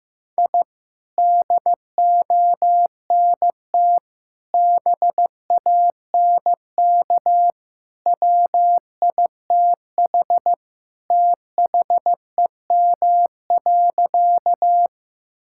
Morse code, heard as a keyed tone, about 15 wpm, 700 Hz. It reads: I DONT BANK WITH THEM.
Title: Sentences from Top 1000 Words - Review Part 6 - 15wpm